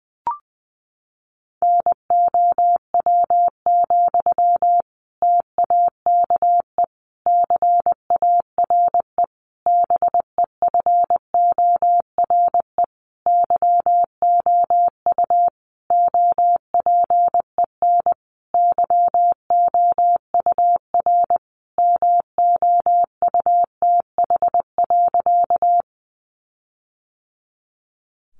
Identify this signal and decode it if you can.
Morse code, heard as a keyed tone, about 20 wpm, 700 Hz. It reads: NOW, TAKE CARE BEFORE YOU OPEN YOUR MOUTH.
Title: Sentences from Top 1500 Words 20wpm